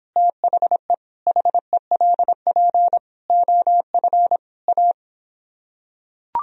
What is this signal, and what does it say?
Morse code, heard as a keyed tone, about 26 wpm, 700 Hz. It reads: THE HELP OF A